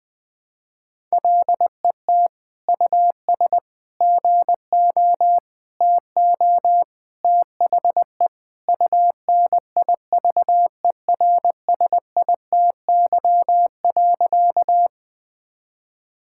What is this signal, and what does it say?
Morse code, heard as a keyed tone, about 20 wpm, 700 Hz. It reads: LET US GO TO THE UNIVERSITY.
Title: Sentences from Top 1000 Words - Review Part 5 - 20wpm